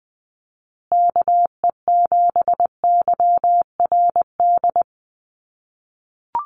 Morse code, heard as a keyed tone, about 20 wpm, 700 Hz. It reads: KE7YRD
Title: USA Call Signs Repeated 2x 20wpm